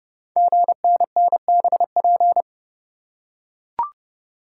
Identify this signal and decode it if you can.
Morse code, heard as a keyed tone, about 28 wpm, 700 Hz. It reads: GNNBP
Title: Code Groups - 5 Letters - 30wpm